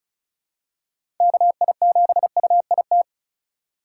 Morse code, heard as a keyed tone, about 36 wpm, 700 Hz. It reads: KI7UIT
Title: USA Call Signs - 6 Characters - 3x Character Spacing - 35wpm